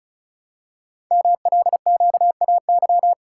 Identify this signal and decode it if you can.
Morse code, heard as a keyed tone, about 36 wpm, 690 Hz. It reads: MLQAY
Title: Code Groups - 5 Letters - 35wpm